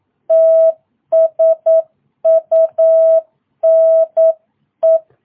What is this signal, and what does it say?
Morse code, heard as a keyed tone, about 9 wpm, 660 Hz. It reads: TSUNE
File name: jh8cbh / 2022cw